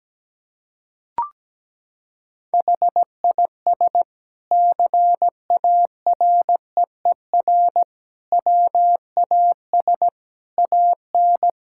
Morse code, heard as a keyed tone, about 17 wpm, 700 Hz. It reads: HIS CAREER WAS AN
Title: Sentences from Top 1000 Words - Review Part 6 - 17wpm